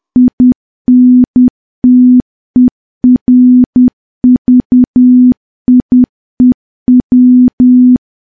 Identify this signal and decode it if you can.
Morse code, heard as a keyed tone, about 10 wpm, 250 Hz. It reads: INTERVIEW
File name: morse / morse-training-10WPM